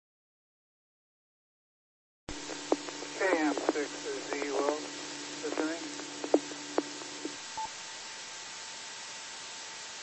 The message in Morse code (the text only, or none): none